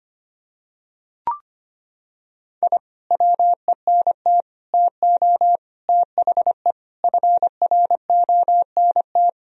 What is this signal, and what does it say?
Morse code, heard as a keyed tone, about 25 wpm, 700 Hz. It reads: I WENT TO THE FRONT
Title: Sentences from Top 1000 Words - Review Part 2 - 25wpm